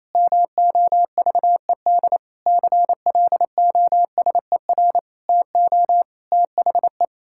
Morse code, heard as a keyed tone, about 28 wpm, 700 Hz. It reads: MOVED CLOSER TO THE
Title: Pangram Challenge v2 - 28wpm